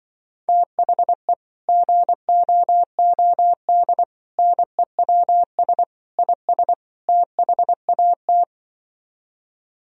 Morse code, heard as a keyed tone, about 24 wpm, 700 Hz. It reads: THE GOOD NEWS IS THAT
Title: Speed-Racing - Sets of 5 Words 20wpm